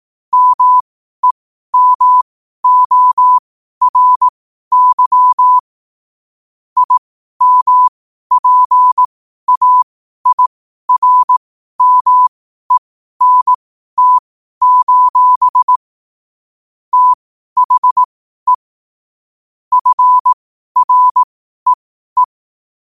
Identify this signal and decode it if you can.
Morse code, heard as a keyed tone, about 18 wpm, spaced as for 12 wpm, 1000 Hz. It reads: MEMORY IMPAIRMENT: THE FREE
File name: qotdfiles / 12wpm